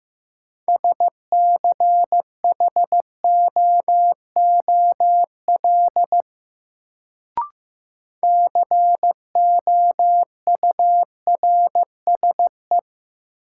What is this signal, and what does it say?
Morse code, heard as a keyed tone, about 15 wpm, 700 Hz. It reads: SCHOOL COURSE
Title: Sets of 2 Words - New Word Drill - 15wpm